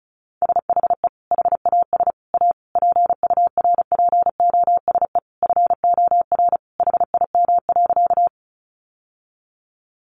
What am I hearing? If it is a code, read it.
Morse code, heard as a keyed tone, about 34 wpm, 700 Hz. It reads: SHE HAS A PURPOSE FOR HIM.
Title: Sentences from Top 800 Words 35wpm